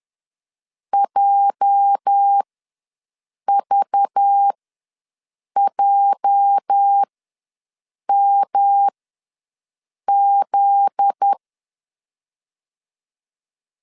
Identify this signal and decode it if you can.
Morse code, heard as a keyed tone, about 10 wpm, spaced as for 6 wpm, 800 Hz. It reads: JVJMZ
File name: LEZIONI / 8 YJ45 / 8 YJ ws 10wpm 4043